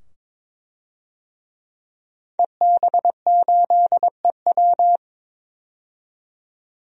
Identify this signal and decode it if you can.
Morse code, heard as a keyed tone, about 22 wpm, 700 Hz. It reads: EB8EW
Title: DXCC-100 Call Signs - 5 Characters - 22wpm